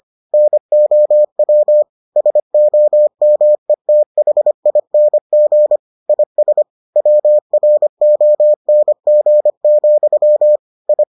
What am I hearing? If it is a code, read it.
Morse code, heard as a keyed tone, about 25 wpm, 600 Hz. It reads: NOW SOMETHING IS WRONG, I